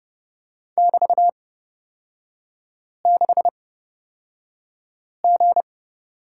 Morse code, heard as a keyed tone, about 30 wpm, spaced as for 5 wpm, 700 Hz. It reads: <BT>6G
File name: Lesson 36e - BT - Three Characters 30wpm